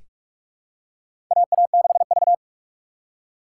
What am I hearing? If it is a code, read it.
Morse code, heard as a keyed tone, about 46 wpm, 700 Hz. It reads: AA6V
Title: USA Call Signs - 4 Characters - 6x Character Spacing - 45wpm